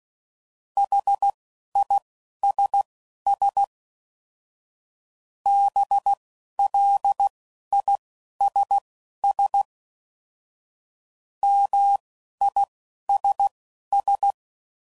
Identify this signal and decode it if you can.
Morse code, heard as a keyed tone, about 16 wpm, spaced as for 11 wpm, 800 Hz. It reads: HISS BLISS MISS